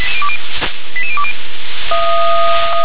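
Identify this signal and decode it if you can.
Five-tone selective call (ZVEI), heard as quick runs of five tones, tones of about 80 ms. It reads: 88020
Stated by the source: ZVEI five-tone selective call received on an SDR.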